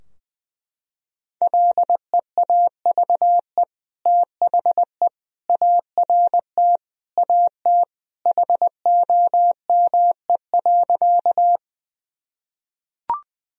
Morse code, heard as a keyed tone, about 19 wpm, 700 Hz. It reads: LEAVE THE ART AT HOME.
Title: Sentences from Top 1000 Words - Review Part 3 - 20wpm